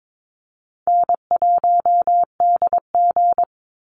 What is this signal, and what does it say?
Morse code, heard as a keyed tone, about 22 wpm, 700 Hz. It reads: N1DG
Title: Easy Call Signs Repeated 3x - 22wpm